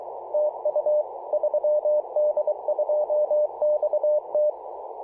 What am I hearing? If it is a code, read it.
Morse code, heard as a keyed tone, about 22 wpm, 600 Hz. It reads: TU 3D2XT
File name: posts / apr-13-2119z-3d2xt-21-021mhz-cw